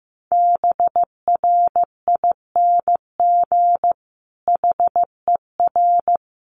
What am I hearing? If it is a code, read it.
Morse code, heard as a keyed tone, about 15 wpm, 700 Hz. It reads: BRING HER
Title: Sentences from Top 1000 Words 15wpm